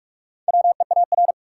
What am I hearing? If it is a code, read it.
Morse code, heard as a keyed tone, about 45 wpm, 700 Hz. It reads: WEAR
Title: Sentences from Top 1500 Words - New Word Drill Alt - 45wpm